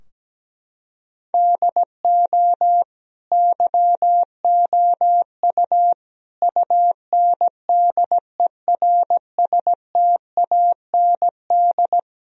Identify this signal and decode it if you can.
Morse code, heard as a keyed tone, about 17 wpm, 700 Hz. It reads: DO YOU UNDERSTAND
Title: Sentences from Top 1000 Words - Review Part 3 - 17wpm